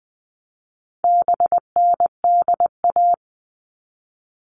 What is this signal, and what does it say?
Morse code, heard as a keyed tone, about 20 wpm, 700 Hz. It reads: BNDA
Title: All Cut Numbers - 4-Digit 20wpm